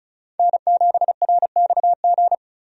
Morse code, heard as a keyed tone, about 36 wpm, 700 Hz. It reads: N7RXG
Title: USA Call Signs - 5 Characters - 35wpm